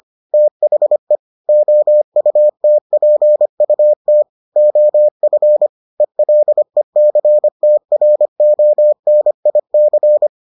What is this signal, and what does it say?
Morse code, heard as a keyed tone, about 25 wpm, 600 Hz. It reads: THE OUTPUT OF ELECTRONIC